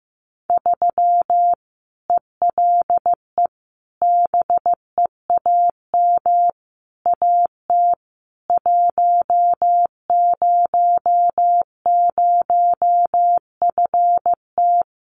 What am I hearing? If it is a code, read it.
Morse code, heard as a keyed tone, about 15 wpm, 700 Hz. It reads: 3 ELE BEAM AT 100FT
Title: Common QSO Phrases 15wpm